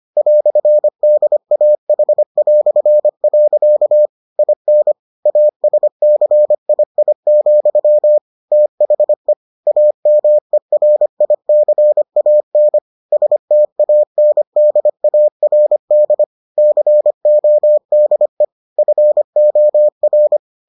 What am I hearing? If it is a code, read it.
Morse code, heard as a keyed tone, about 25 wpm, 600 Hz. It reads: "DAH". IN ASCII, THE AMERICAN STANDARD CODE FOR